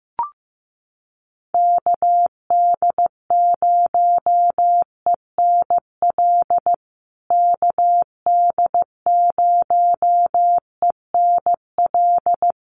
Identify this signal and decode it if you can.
Morse code, heard as a keyed tone, about 15 wpm, 700 Hz. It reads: KD0ENL KD0ENL
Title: USA Call Signs Repeated 3x 15wpm